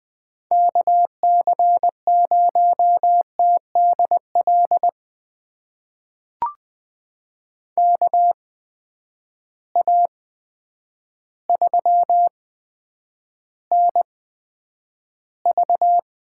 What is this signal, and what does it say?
Morse code, heard as a keyed tone, about 21 wpm, spaced as for 9 wpm, 700 Hz. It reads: KC0TDL KA3NV
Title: USA Call Signs - 6 Characters - 8x Character Spacing - 20wpm